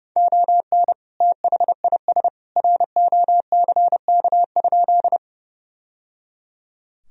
Morse code, heard as a keyed tone, about 30 wpm, 700 Hz. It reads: ON THIS ROCK?
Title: Sentences from Top 1000 Words - Review Part 2 - 30wpm